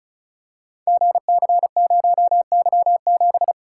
Morse code, heard as a keyed tone, about 36 wpm, 700 Hz. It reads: GC0Y7
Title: Code Groups - 5 Letter:Numbers - 35wpm